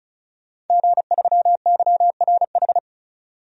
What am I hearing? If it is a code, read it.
Morse code, heard as a keyed tone, about 34 wpm, 700 Hz. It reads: G3YRH
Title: DXCC-100 Call Signs - 5 Characters - 35wpm